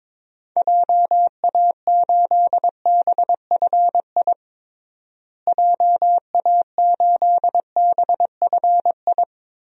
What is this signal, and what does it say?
Morse code, heard as a keyed tone, about 22 wpm, 700 Hz. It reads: JA8BFI JA8BFI
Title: DXCC-100 Call Signs - 6 Characters - Repeated 3x - 22wpm